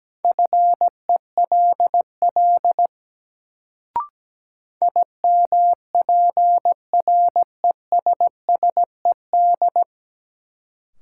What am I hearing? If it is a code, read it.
Morse code, heard as a keyed tone, about 17 wpm, 700 Hz. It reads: FELL IMPRESSED
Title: Sentences from Top 1600 Words - New Word Drill - 17wpm